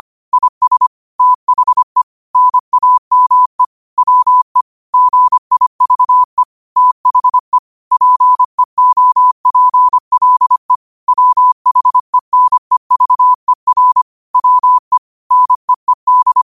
Morse code, heard as a keyed tone, about 25 wpm, 1000 Hz. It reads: IS THE NAME WE GIVE THE PEOPLE WHENEVER WE NEED
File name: qotdfiles / 25wpm